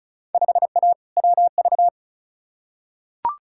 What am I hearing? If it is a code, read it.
Morse code, heard as a keyed tone, about 36 wpm, 700 Hz. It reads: 5A WV
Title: Field Day Exchanges - 35wpm